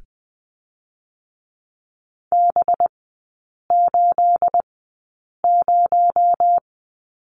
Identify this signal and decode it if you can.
Morse code, heard as a keyed tone, about 20 wpm, 700 Hz. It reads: B 8 0
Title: 218 Lesson 35e - Number 0 - Three Characters 20wpm